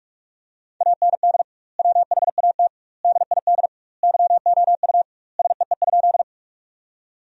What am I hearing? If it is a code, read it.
Morse code, heard as a keyed tone, about 46 wpm, 700 Hz. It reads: AND WHAT DID YOU SEE?
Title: 2x Word-Spacing - Sentences from Top 100 Words - 45wpm